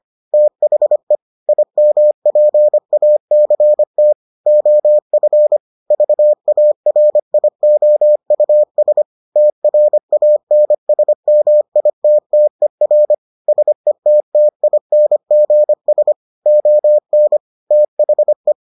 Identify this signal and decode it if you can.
Morse code, heard as a keyed tone, about 25 wpm, 600 Hz. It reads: THE IMPACT OF VARIOUS TRANSMITTER SETTINGS ON THE